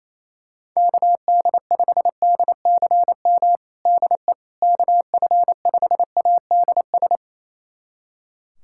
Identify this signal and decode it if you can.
Morse code, heard as a keyed tone, about 28 wpm, 700 Hz. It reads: KD5DCM DE KF5ADS
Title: USA Callsign DE Callsign 28wpm